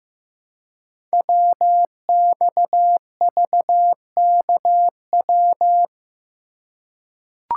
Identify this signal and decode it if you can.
Morse code, heard as a keyed tone, about 15 wpm, 700 Hz. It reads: WXVKW
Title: Code Groups - 5 Letters - 15wpm